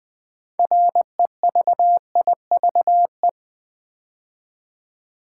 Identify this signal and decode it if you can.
Morse code, heard as a keyed tone, about 20 wpm, 700 Hz. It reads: REVIVE
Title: Build A Word - 2 Letters At A Time - 6-Letter Words - 20wpm